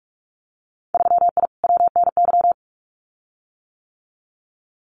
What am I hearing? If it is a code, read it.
Morse code, heard as a keyed tone, about 46 wpm, 700 Hz. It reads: 3I WNY
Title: Winter Field Day Exchanges Encore 45wpm